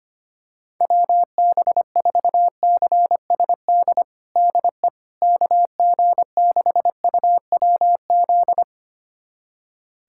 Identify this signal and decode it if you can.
Morse code, heard as a keyed tone, about 25 wpm, 700 Hz. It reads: WB4CSD DE KG6UWZ
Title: USA Callsign DE Callsign 25wpm